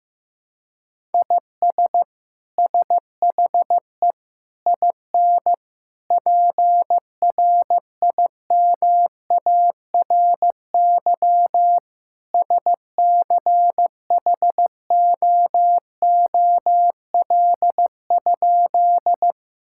Morse code, heard as a keyed tone, about 15 wpm, 700 Hz. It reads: IS SHE IN PRIMARY SCHOOL?
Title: Sentences from Top 1000 Words - Review Part 5 - 15wpm